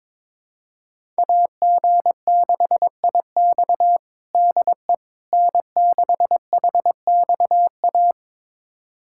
Morse code, heard as a keyed tone, about 22 wpm, 700 Hz. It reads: AG6IX DE N6HXA
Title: USA Callsign DE Callsign - 5 Characters - 22wpm